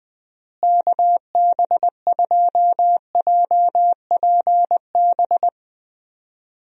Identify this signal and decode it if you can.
Morse code, heard as a keyed tone, about 21 wpm, 700 Hz. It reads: KB2JPB
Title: USA Call Signs - 6 Characters - Repeated 3x - 20wpm